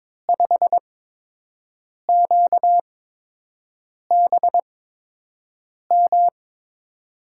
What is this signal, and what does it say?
Morse code, heard as a keyed tone, about 22 wpm, spaced as for 6 wpm, 700 Hz. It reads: 5QBM
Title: USA Call Signs - 5 Characters - 8x Character Spacing - 22wpm